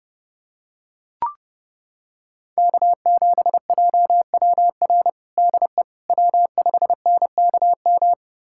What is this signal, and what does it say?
Morse code, heard as a keyed tone, about 32 wpm, 700 Hz. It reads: K7JWR DE W5NKM
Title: USA Callsign DE Callsign 30wpm